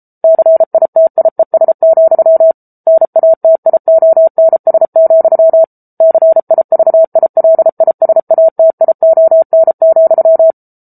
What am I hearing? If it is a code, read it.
Morse code, heard as a keyed tone, about 34 wpm, 650 Hz. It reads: CITIES, NATIONS, CIVILISATION,